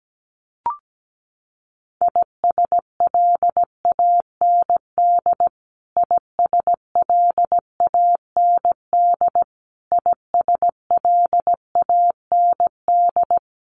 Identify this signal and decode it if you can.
Morse code, heard as a keyed tone, about 17 wpm, 700 Hz. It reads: ISLAND ISLAND ISLAND
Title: Top 500 Words Repeated 17wpm